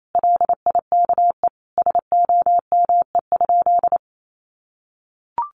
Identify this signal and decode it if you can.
Morse code, heard as a keyed tone, about 28 wpm, 700 Hz. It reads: LIKE SOME?
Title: Sentences from Top 100 Words 28wpm